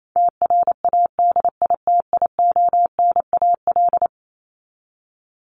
Morse code, heard as a keyed tone, about 28 wpm, 700 Hz. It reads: TRADITIONAL
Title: Top 751-1000 Words 28wpm